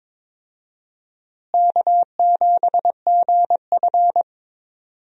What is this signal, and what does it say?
Morse code, heard as a keyed tone, about 22 wpm, 700 Hz. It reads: K7GF
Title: USA Call Signs - 4 Characters - Repeated 3x - 22wpm